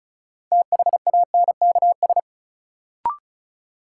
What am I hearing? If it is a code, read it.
Morse code, heard as a keyed tone, about 36 wpm, 700 Hz. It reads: THANKS
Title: Sets of 5 Words - New Word Drill - 35wpm